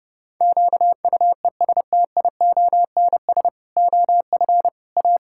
Morse code, heard as a keyed tone, about 30 wpm, 700 Hz. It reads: QUESTIONS OF A
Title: Pangram Challenge v2 - 30wpm